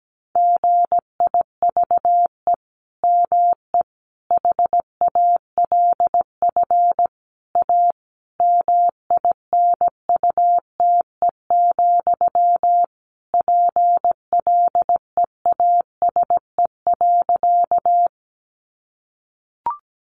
Morse code, heard as a keyed tone, about 17 wpm, 700 Hz. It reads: GIVE ME HALF A MINUTE, PLEASE.
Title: Sentences from Top 1500 Words 17wpm